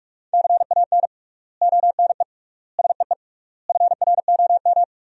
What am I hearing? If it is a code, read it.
Morse code, heard as a keyed tone, about 41 wpm, 690 Hz. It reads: CAN ONE SEE FROM
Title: 3x Word-Spacing - Sentences from Top 100 Words - 45wpm